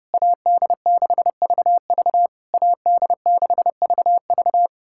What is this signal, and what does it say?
Morse code, heard as a keyed tone, about 30 wpm, 700 Hz. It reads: AD6VV AD6VV
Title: SOTA QSO Practice 30wpm